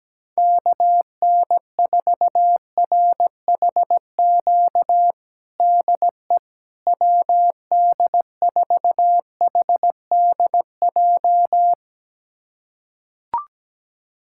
Morse code, heard as a keyed tone, about 17 wpm, 700 Hz. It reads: KN4RHQ DE WD4HDJ
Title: USA Callsign DE Callsign - 6 Characters - 17wpm